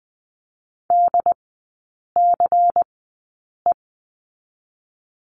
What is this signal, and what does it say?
Morse code, heard as a keyed tone, about 20 wpm, spaced as for 8 wpm, 700 Hz. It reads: DCE
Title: 172 Lesson 28e - Number 9 - Three Characters 20wpm